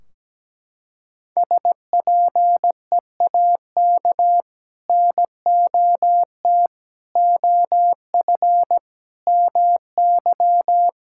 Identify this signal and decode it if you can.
Morse code, heard as a keyed tone, about 17 wpm, 700 Hz. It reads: SPEAK NOT OF MY
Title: Sentences from Top 1600 Words 17wpm